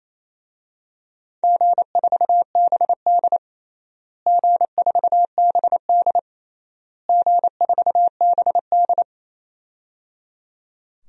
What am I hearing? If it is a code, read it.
Morse code, heard as a keyed tone, about 27 wpm, 700 Hz. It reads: G4BD G4BD G4BD
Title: DXCC-100 Call Signs - 4 Characters - Repeated 3x - 28wpm